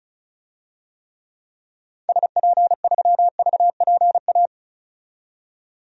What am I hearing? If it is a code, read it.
Morse code, heard as a keyed tone, about 34 wpm, 700 Hz. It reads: SP3VPA
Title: DXCC-100 Call Signs - 6 Characters - 35wpm